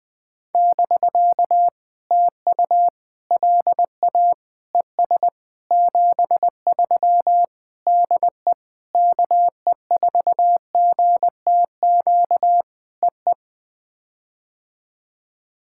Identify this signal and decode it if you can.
Morse code, heard as a keyed tone, about 19 wpm, 700 Hz. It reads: <BK> TU LA ES 73 DE KE4GTQ EE
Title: POTA QSO Practice - 20wpm